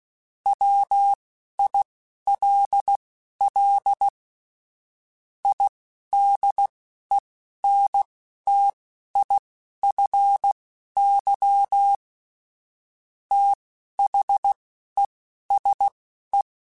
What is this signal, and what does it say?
Morse code, heard as a keyed tone, about 16 wpm, spaced as for 11 wpm, 800 Hz. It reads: WILL IDENTIFY THESE
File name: Technical Subjects / Technical Subjects-mp3s / Sw Pwr Sup @10